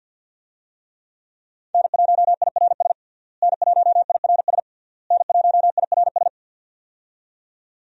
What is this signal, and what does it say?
Morse code, heard as a keyed tone, about 50 wpm, 700 Hz. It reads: N1IRS N1IRS N1IRS
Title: USA Call Signs - 5 Characters - Repeated 3x - 50wpm